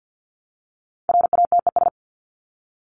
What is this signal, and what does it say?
Morse code, heard as a keyed tone, about 50 wpm, 700 Hz. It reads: RATES
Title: ES:S Suffix - Top 100 Words with Base Words 50wpm